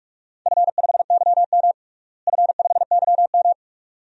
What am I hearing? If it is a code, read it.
Morse code, heard as a keyed tone, about 46 wpm, 700 Hz. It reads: F5YM F5YM
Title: DXCC-100 Call Signs - 4 Characters - Repeated 2x - 45wpm